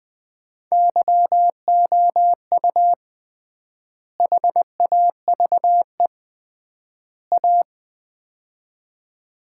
Morse code, heard as a keyed tone, about 21 wpm, 700 Hz. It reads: YOU HAVE A
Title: 3x Word-Spacing - Sets of 3 Words - 20wpm